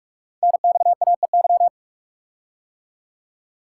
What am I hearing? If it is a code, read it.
Morse code, heard as a keyed tone, about 44 wpm, 700 Hz. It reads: NXAEY
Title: Code Groups - 5 Letters - 45wpm